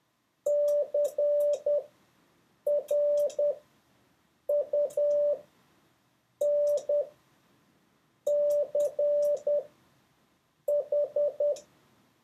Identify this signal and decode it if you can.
Morse code, heard as a keyed tone, about 10 wpm, spaced as for 6 wpm, 580 Hz. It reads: CRUNCH
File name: 4of6-words-1